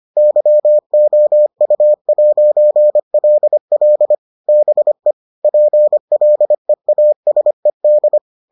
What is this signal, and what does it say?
Morse code, heard as a keyed tone, about 25 wpm, 600 Hz. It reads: YOU'LL BE PLEASED